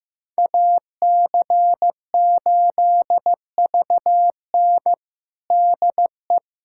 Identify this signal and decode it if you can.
Morse code, heard as a keyed tone, about 16 wpm, 700 Hz. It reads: AC8VN DE
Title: USA Callsign DE Callsign - 5 Characters - 15wpm